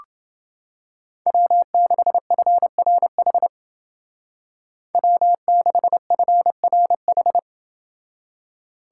Morse code, heard as a keyed tone, about 29 wpm, 700 Hz. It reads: W6FRH W6FRH